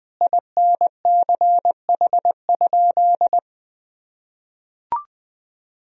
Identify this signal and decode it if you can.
Morse code, heard as a keyed tone, about 20 wpm, 700 Hz. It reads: INCH?
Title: Sentences from Top 1000 Words - Review Part 3 - 20wpm